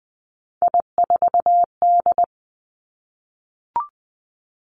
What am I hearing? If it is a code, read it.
Morse code, heard as a keyed tone, about 20 wpm, 700 Hz. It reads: I4D
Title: Three Challenging Letter-Number Combinations 20wpm